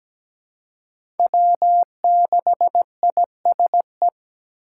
Morse code, heard as a keyed tone, about 17 wpm, 700 Hz. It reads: W6ISE
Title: USA Call Signs - 5 Characters - 4x Character Spacing - 17wpm